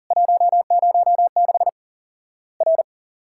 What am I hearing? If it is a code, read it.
Morse code, heard as a keyed tone, about 40 wpm, 700 Hz. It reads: 106 R
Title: Worked All Europe DX Contest - QTC Exchange - 40wpm